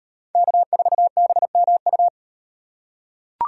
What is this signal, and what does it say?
Morse code, heard as a keyed tone, about 38 wpm, 700 Hz. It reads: K4BMU